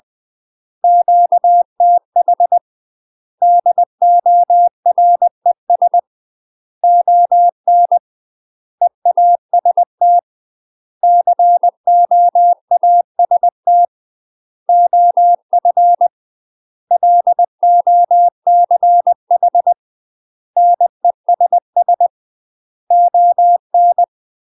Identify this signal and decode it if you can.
Morse code, heard as a keyed tone, about 20 wpm, 700 Hz. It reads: QTH DORES ON EAST COAST OF LOCH NESS ON
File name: UK MASTER QSOs TRAINING / 20WPM / UK MASTER QSOs 0062